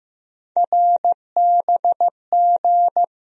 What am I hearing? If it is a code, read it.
Morse code, heard as a keyed tone, about 15 wpm, 700 Hz. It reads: RBG
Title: USA Call Sign Postfixes - 3 Letters 15wpm